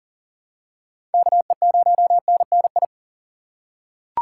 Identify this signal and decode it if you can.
Morse code, heard as a keyed tone, about 40 wpm, 700 Hz. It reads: KE0NNI
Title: USA Call Signs - 6 Characters - 3x Character Spacing - 40wpm